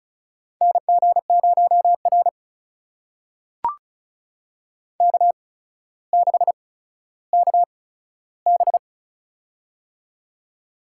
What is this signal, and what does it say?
Morse code, heard as a keyed tone, about 36 wpm, spaced as for 15 wpm, 700 Hz. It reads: NG0R K6KB
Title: USA Call Signs - 8x Character Spacing - 35wpm